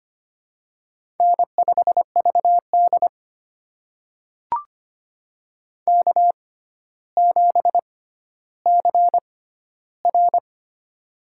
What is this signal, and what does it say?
Morse code, heard as a keyed tone, about 25 wpm, spaced as for 13 wpm, 700 Hz. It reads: N5VD K7CR
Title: USA Call Signs - 4 Characters - 6x Character Spacing - 25wpm